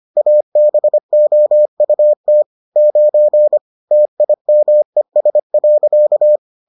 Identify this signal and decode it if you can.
Morse code, heard as a keyed tone, about 25 wpm, 600 Hz. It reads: ABOUT 9 TIMES.